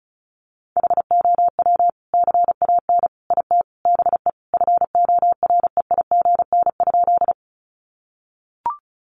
Morse code, heard as a keyed tone, about 34 wpm, 700 Hz. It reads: HOW CAN IT BE FOREIGN?
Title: Sentences from Top 1400 Words 35wpm